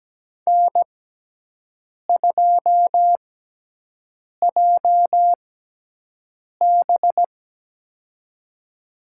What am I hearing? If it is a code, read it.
Morse code, heard as a keyed tone, about 17 wpm, spaced as for 6 wpm, 700 Hz. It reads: N2JB
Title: USA Call Signs - 4 Characters - 6x Character Spacing - 17wpm